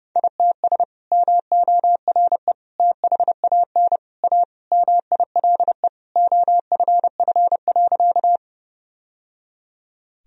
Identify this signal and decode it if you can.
Morse code, heard as a keyed tone, about 30 wpm, 700 Hz. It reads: ITS MORE THAN A MILE OFF.
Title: Sentences from Top 1000 Words - Review Part 1 - 30wpm